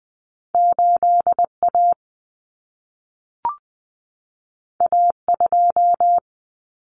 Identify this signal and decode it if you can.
Morse code, heard as a keyed tone, about 20 wpm, 700 Hz. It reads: 8A A2
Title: Common Cut Numbers - 2-Digit 20wpm